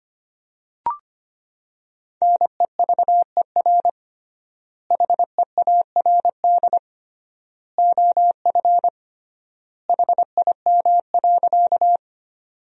Morse code, heard as a keyed tone, about 25 wpm, 700 Hz. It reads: NEVER HEARD OF HIM.
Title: 3x Word-Spacing - Sentences from Top 400 Words - 25wpm